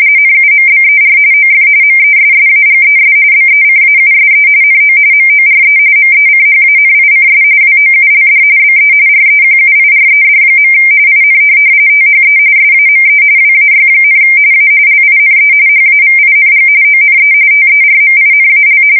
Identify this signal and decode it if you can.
RTTY, about 45.45 baud. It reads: NUMBER OF MESSAGES BECAUSE NO MORE MECHANICS WOULD BE REQUIRED. IN THE SAME WAY, A LINE HAVING ONE TERMINAL
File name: fsk170 / printing-telegraph